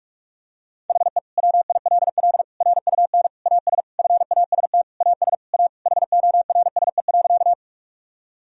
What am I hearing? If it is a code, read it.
Morse code, heard as a keyed tone, about 46 wpm, 700 Hz. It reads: HE WILL RUN AS FAST AS A HORSE.
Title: Sentences from Top 1000 Words - Review Part 2 - 45wpm